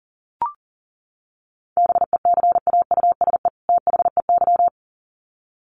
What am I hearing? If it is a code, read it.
Morse code, heard as a keyed tone, about 40 wpm, 700 Hz. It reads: BECAUSE THEY
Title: Sets of 2 Words 40wpm